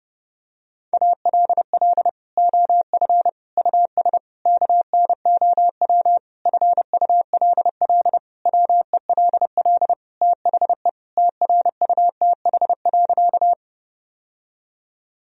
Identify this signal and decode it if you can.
Morse code, heard as a keyed tone, about 30 wpm, 700 Hz. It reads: ALL OF US KNOW FULL WELL THE TRUTH.
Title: Sentences from Top 1000 Words - Review Part 3 - 30wpm